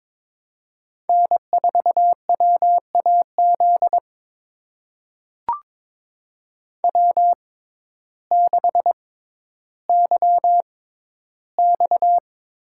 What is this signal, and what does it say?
Morse code, heard as a keyed tone, about 22 wpm, spaced as for 12 wpm, 700 Hz. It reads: N4WAZ W6YX
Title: USA Call Signs - 6x Character Spacing - 22wpm